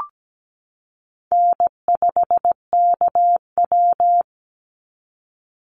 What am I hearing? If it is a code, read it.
Morse code, heard as a keyed tone, about 17 wpm, 700 Hz. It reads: N5KW
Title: Call Signs Repeated 3x - 17wpm